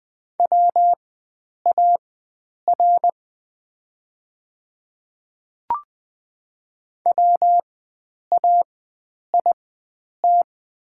Morse code, heard as a keyed tone, about 20 wpm, spaced as for 9 wpm, 700 Hz. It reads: WAR WAIT
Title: 4x Character Spacing - Top 500 Words - 20wpm